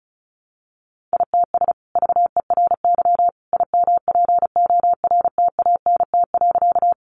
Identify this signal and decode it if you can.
Morse code, heard as a keyed tone, about 34 wpm, 700 Hz. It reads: ITS VERY IMPORTANT.
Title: Sentences from Top 600 Words 35wpm